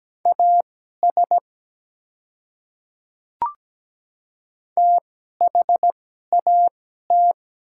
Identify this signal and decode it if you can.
Morse code, heard as a keyed tone, about 17 wpm, spaced as for 12 wpm, 700 Hz. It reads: AS THAT
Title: 2x Character Spacing - Top 100 Words - 17wpm